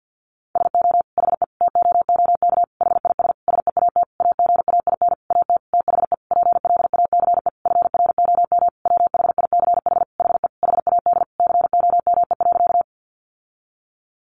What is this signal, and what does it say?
Morse code, heard as a keyed tone, about 50 wpm, 700 Hz. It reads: SO HE TOOK HIS SEAT AGAIN AT THE PLACE FROM WHICH HE HAD COME.
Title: Sentences from Top 1500 Words 50wpm